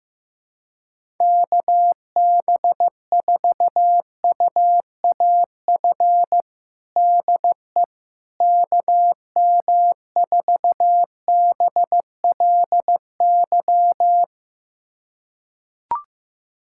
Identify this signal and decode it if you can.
Morse code, heard as a keyed tone, about 15 wpm, 700 Hz. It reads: KB4UAF DE KM4BLY
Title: USA Callsign DE Callsign 15wpm